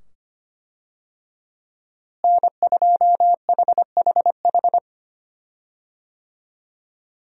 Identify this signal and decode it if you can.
Morse code, heard as a keyed tone, about 25 wpm, 700 Hz. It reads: N2HHH
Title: USA Call Signs - 5 Characters - Repeated 2x - 25wpm